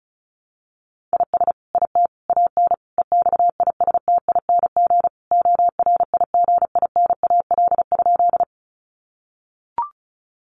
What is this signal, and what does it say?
Morse code, heard as a keyed tone, about 34 wpm, 700 Hz. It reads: IS IT AN EXISTING ORIGINAL?
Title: Sentences from Top 1500 Words 35wpm